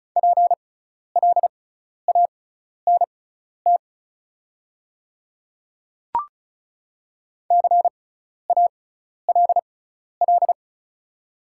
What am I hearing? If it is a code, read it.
Morse code, heard as a keyed tone, about 34 wpm, spaced as for 12 wpm, 700 Hz. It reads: PLANT CALL
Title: 6x Character Spacing - Top 300 Words - 35wpm